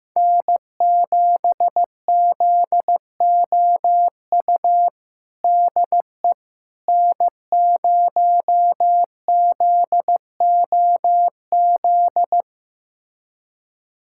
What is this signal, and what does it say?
Morse code, heard as a keyed tone, about 15 wpm, 700 Hz. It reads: N7ZOU DE N0ZOZ
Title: USA Callsign DE Callsign - 5 Characters - 15wpm